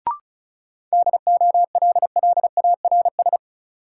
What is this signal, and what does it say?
Morse code, heard as a keyed tone, about 34 wpm, 700 Hz. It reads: DOLLARS
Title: Sets of 5 Words - New Word Drill - 35wpm